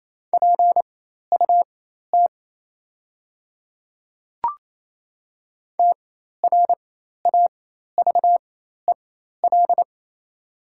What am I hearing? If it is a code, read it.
Morse code, heard as a keyed tone, about 28 wpm, spaced as for 13 wpm, 700 Hz. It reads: PUT TRAVEL
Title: 4x Character Spacing - Top 500 Words - 28wpm